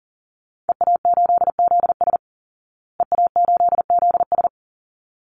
Morse code, heard as a keyed tone, about 40 wpm, 700 Hz. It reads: EA8ZS EA8ZS
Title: Call Signs Repeated 3x - 40wpm